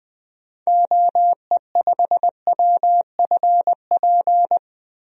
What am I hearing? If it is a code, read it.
Morse code, heard as a keyed tone, about 20 wpm, 700 Hz. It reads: OE5WFP
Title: DXCC-100 Call Signs - 6 Characters - Repeated 2x - 20wpm